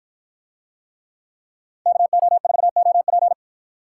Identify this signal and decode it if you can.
Morse code, heard as a keyed tone, about 53 wpm, 700 Hz. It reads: KO4OP